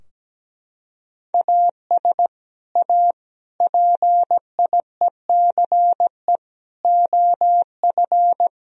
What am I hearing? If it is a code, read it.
Morse code, heard as a keyed tone, about 17 wpm, 700 Hz. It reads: AS A PIECE OF
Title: Sentences from Top 1000 Words - Review Part 3 - 17wpm